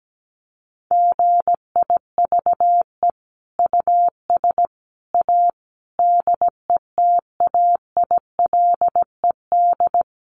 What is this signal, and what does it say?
Morse code, heard as a keyed tone, about 17 wpm, 700 Hz. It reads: GIVE US A DETAILED
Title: Sentences from Top 1500 Words 17wpm